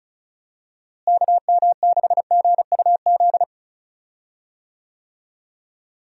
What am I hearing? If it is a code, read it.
Morse code, heard as a keyed tone, about 36 wpm, 700 Hz. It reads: KM6GUZ
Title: USA Call Signs - 6 Characters - Repeated 2x - 35wpm